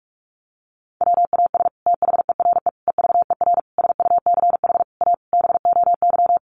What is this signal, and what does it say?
Morse code, heard as a keyed tone, about 44 wpm, 700 Hz. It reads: WAS THERE EVER SUCH A BOY
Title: Sentences from Top 400 Words 45wpm